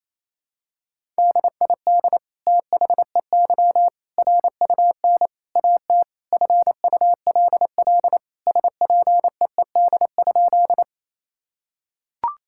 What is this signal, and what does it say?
Morse code, heard as a keyed tone, about 28 wpm, 700 Hz. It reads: DID THEY RUN AT FULL SPEED?
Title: Sentences from Top 1000 Words - Review Part 5 - 28wpm